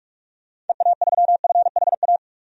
Morse code, heard as a keyed tone, about 46 wpm, 700 Hz. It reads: EA3FHA
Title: DXCC-100 Call Signs - 6 Characters - Repeated 3x - 45wpm